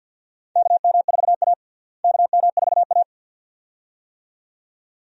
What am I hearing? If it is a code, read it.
Morse code, heard as a keyed tone, about 52 wpm, 700 Hz. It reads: KM4A KM4A
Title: USA Call Signs - 4 Characters - Repeated 2x - 50wpm